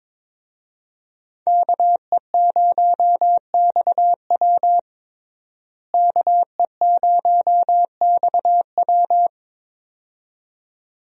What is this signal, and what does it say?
Morse code, heard as a keyed tone, about 22 wpm, 700 Hz. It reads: KE0XW KE0XW
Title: USA Call Signs - 5 Characters - Repeated 2x - 22wpm